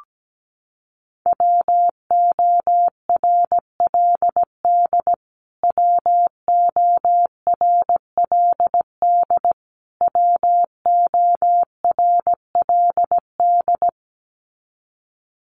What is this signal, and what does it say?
Morse code, heard as a keyed tone, about 17 wpm, 700 Hz. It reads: WORLD WORLD WORLD
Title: Top 500 Words Repeated 17wpm